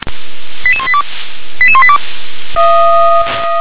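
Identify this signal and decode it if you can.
Five-tone selective call (ZVEI), heard as quick runs of five tones, tones of about 70 ms. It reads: 88182 88182
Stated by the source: ZVEI five-tone call received by SDR.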